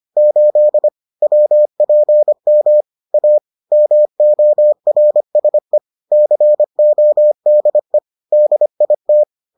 Morse code, heard as a keyed tone, about 25 wpm, 600 Hz. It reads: 8 WPM A MORSE CODE DIT